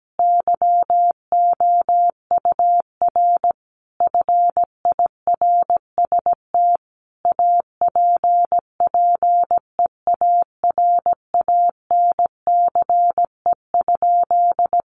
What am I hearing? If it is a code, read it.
Morse code, heard as a keyed tone, about 17 wpm, 700 Hz. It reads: YOUR FIRST APPEARANCE?
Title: Sentences from Top 1100 Words 17wpm